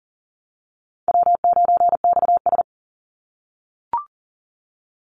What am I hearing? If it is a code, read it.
Morse code, heard as a keyed tone, about 40 wpm, 700 Hz. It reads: W9XS
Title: Call Signs - 40wpm